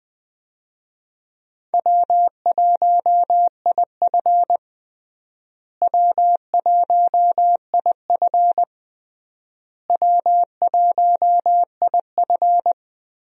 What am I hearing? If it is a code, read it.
Morse code, heard as a keyed tone, about 20 wpm, 700 Hz. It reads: W1IF W1IF W1IF
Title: USA Call Signs - 4 Characters - Repeated 3x - 20wpm